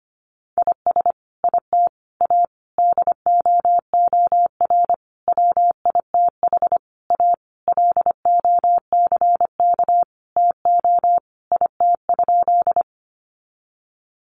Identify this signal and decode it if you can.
Morse code, heard as a keyed tone, about 25 wpm, 700 Hz. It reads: IS IT A DOOR WITH A LOCK TO IT?
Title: Sentences from Top 1300 Words 25wpm